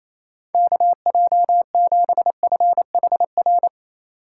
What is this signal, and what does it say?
Morse code, heard as a keyed tone, about 28 wpm, 700 Hz. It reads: KJ7FHR
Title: USA Call Signs - 6 Characters - 3x Character Spacing - 28wpm